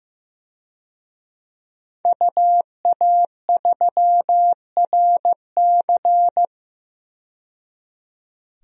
Morse code, heard as a keyed tone, about 15 wpm, 700 Hz. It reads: UA3RC
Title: DXCC-100 Call Signs - 5 Characters - 15wpm